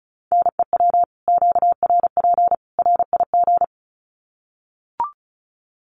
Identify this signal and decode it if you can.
Morse code, heard as a keyed tone, about 34 wpm, 700 Hz. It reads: NEW QRP RIG
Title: Common QSO Phrases 35wpm